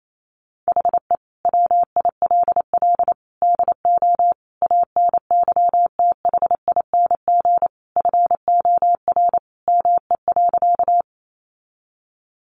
Morse code, heard as a keyed tone, about 28 wpm, 700 Hz. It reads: HE WILL DO ANYTHING FOR ME.
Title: Sentences from Top 1000 Words 28wpm